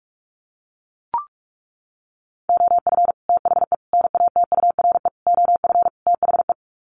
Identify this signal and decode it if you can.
Morse code, heard as a keyed tone, about 44 wpm, 700 Hz. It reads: OF THE NATURE OF THE
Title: Sets of 5 Words Encore2 45wpm